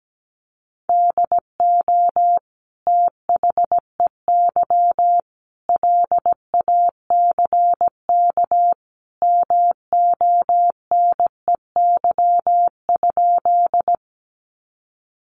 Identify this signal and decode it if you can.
Morse code, heard as a keyed tone, about 17 wpm, 700 Hz. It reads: DO THEY LACK MONEY?
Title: Sentences from Top 1100 Words 17wpm